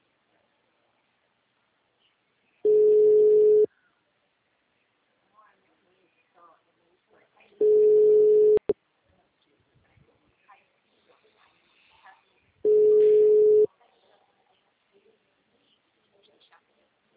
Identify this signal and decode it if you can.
Ringback tone (Europe, 425 Hz), with speech.